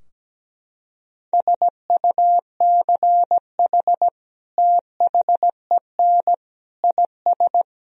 Morse code, heard as a keyed tone, about 17 wpm, 700 Hz. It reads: SUCH THEN IS
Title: Sentences from Top 1000 Words - Review Part 4 - 17wpm